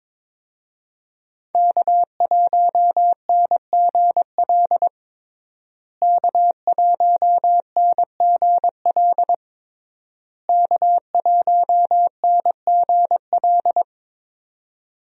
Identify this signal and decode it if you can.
Morse code, heard as a keyed tone, about 22 wpm, 700 Hz. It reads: K1NGL K1NGL K1NGL
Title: USA Call Signs - 5 Characters - Repeated 3x - 22wpm